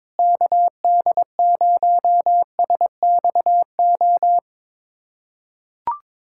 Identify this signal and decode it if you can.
Morse code, heard as a keyed tone, about 21 wpm, 700 Hz. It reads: KD0SXO
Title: USA Callsign DE Callsign 22wpm